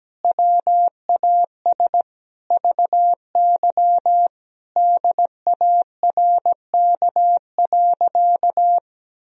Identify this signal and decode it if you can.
Morse code, heard as a keyed tone, about 17 wpm, 700 Hz. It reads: WAS VY DARK.
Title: Sentences from Top 500 Words and Abbreviations 17wpm